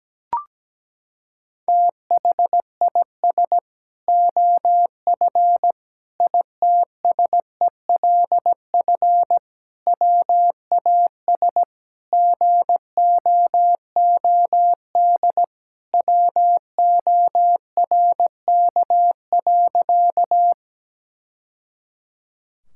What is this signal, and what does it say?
Morse code, heard as a keyed tone, about 17 wpm, 700 Hz. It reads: THIS OF ITSELF WAS GOOD WORK.
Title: Sentences from Top 1000 Words - Review Part 5 - 17wpm